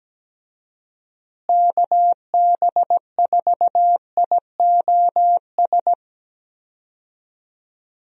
Morse code, heard as a keyed tone, about 17 wpm, 700 Hz. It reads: KB4IOS
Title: USA Call Signs - 6 Characters - Repeated 2x - 17wpm